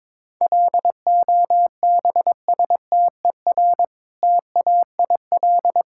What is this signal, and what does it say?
Morse code, heard as a keyed tone, about 22 wpm, 700 Hz. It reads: LOBSTER TAIL
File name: Difficult Words with Hints - 22wpm